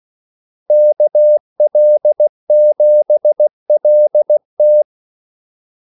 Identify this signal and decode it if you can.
Morse code, heard as a keyed tone, about 16 wpm, 600 Hz. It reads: KL7LT
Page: Kaldesignaler 16 wpm